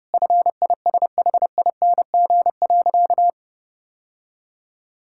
Morse code, heard as a keyed tone, about 30 wpm, 700 Hz. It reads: FISHING.
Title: Sentences from Top 1000 Words - Review Part 5 - 30wpm